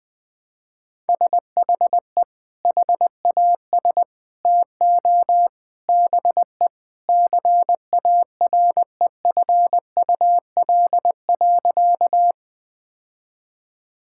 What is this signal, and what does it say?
Morse code, heard as a keyed tone, about 20 wpm, 700 Hz. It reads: SHE HAS TO BE CAREFUL.
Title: Sentences from Top 1600 Words 20wpm